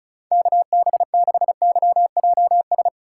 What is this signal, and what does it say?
Morse code, heard as a keyed tone, about 34 wpm, 700 Hz. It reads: KB6YJS
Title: USA Call Signs - 6 Characters - 2x Character Spacing - 35wpm